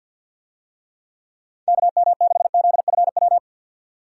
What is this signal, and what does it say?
Morse code, heard as a keyed tone, about 49 wpm, 700 Hz. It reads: KM6ZFW
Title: USA Call Signs - 6 Characters - 50wpm